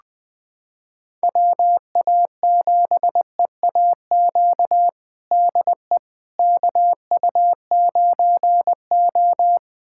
Morse code, heard as a keyed tone, about 20 wpm, 700 Hz. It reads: WA7EAQ DE KU9O
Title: USA Callsign DE Callsign 20wpm